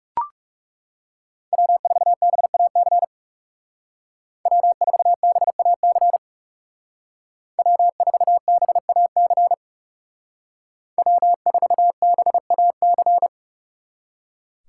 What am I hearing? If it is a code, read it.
Morse code, heard as a keyed tone, about 36 wpm, 700 Hz. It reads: W4BAC W4BAC W4BAC W4BAC
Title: Speed Racing - USA Call Signs 30wpm